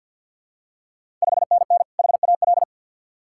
Morse code, heard as a keyed tone, about 49 wpm, 690 Hz. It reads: 5NN HAL
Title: Ontario QSO Party Exchange 50wpm